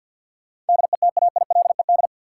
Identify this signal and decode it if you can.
Morse code, heard as a keyed tone, about 53 wpm, 700 Hz. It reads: DETAILED
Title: Sentences from Top 1500 Words - New Word Drill - 50wpm